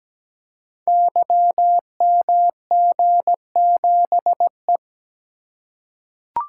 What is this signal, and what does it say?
Morse code, heard as a keyed tone, about 17 wpm, 700 Hz. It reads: YMG7E
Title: Code Groups - 5 Letter:Numbers - 17wpm